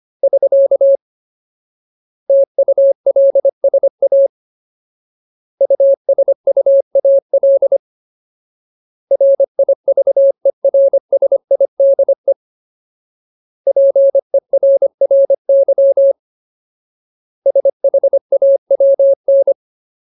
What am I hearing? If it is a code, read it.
Morse code, heard as a keyed tone, about 25 wpm, 550 Hz. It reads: <SK> TULSA USUAL RIVERSIDE PERRY SHAWN
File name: Ultimate Word List / 25WPM Ultimate